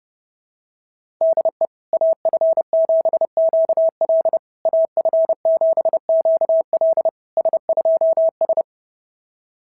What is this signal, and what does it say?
Morse code, heard as a keyed tone, about 30 wpm, 690 Hz. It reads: DE AF7QL AF7QL S2S
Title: SOTA QSO Practice 30wpm